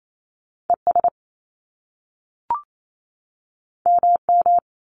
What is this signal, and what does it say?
Morse code, heard as a keyed tone, about 28 wpm, 700 Hz. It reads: ES MM
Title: Top Two Letter Combinations in Words 28wpm